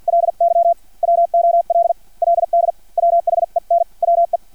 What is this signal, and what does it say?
Morse code, heard as a keyed tone, about 50 wpm, spaced as for 40 wpm, 680 Hz. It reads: LY WOLLD WHEN WE
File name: archived / CRSnet